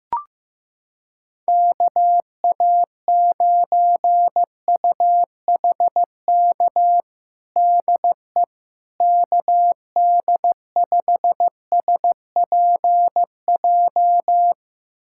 Morse code, heard as a keyed tone, about 16 wpm, 700 Hz. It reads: KA9UHK DE KD5SPJ
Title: USA Callsign DE Callsign - 6 Characters - 15wpm